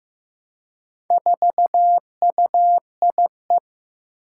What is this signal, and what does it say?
Morse code, heard as a keyed tone, about 15 wpm, 700 Hz. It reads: 4UIE
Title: Four Challenging Letter-Number Combinations 15wpm